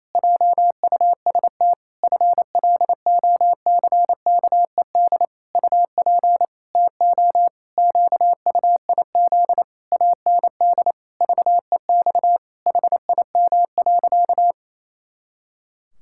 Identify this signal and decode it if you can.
Morse code, heard as a keyed tone, about 28 wpm, 700 Hz. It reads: JUST FLOCKED UP TO QUIZ AND VEX HIM.
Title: Pangram Challenge v2 - 28wpm